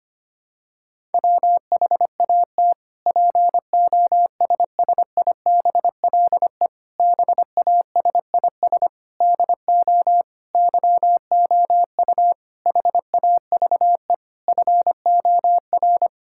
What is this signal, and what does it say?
Morse code, heard as a keyed tone, about 25 wpm, 700 Hz. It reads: WHAT POSSIBLE BASIS DO YOU HAVE FOR
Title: Sentences from Top 1000 Words - Review Part 5 - 25wpm